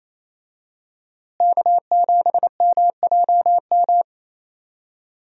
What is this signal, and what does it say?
Morse code, heard as a keyed tone, about 28 wpm, 700 Hz. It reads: K7MJM
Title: USA Call Signs - 5 Characters - Repeated 3x - 28wpm